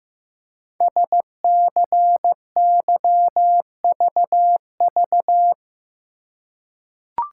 Morse code, heard as a keyed tone, about 15 wpm, 700 Hz. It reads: SCYVV
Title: Code Groups - 5 Letters - 15wpm